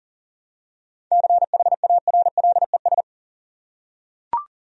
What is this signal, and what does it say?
Morse code, heard as a keyed tone, about 38 wpm, 700 Hz. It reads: CHARLES
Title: USA Common Names - 40wpm